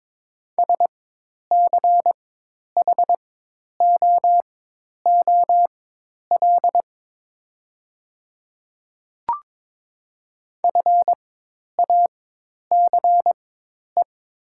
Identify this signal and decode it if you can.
Morse code, heard as a keyed tone, about 22 wpm, spaced as for 10 wpm, 700 Hz. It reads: SCHOOL FACE
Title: 4x Character Spacing - Top 500 Words - 22wpm